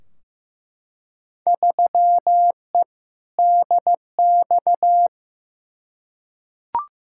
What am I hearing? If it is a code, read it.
Morse code, heard as a keyed tone, about 15 wpm, 700 Hz. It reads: 3E DX
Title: Field Day Exchanges Encore - 15wpm